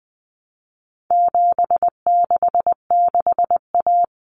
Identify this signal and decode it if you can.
Morse code, heard as a keyed tone, about 20 wpm, 700 Hz. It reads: 766A
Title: Common Cut Numbers - 4-Digit 20wpm